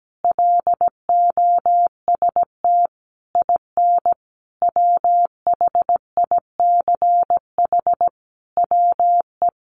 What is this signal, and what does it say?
Morse code, heard as a keyed tone, about 17 wpm, 700 Hz. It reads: LOST IN WHICH WE
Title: Sentences from Top 1400 Words 17wpm